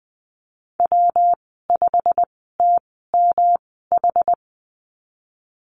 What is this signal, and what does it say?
Morse code, heard as a keyed tone, about 20 wpm, 700 Hz. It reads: W 5 T M H
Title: 238 Lesson 38g - Letter q - Call Signs 20wpm